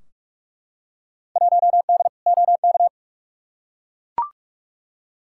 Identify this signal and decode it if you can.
Morse code, heard as a keyed tone, about 44 wpm, 700 Hz. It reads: 1D OK
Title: Field Day Exchanges - 45wpm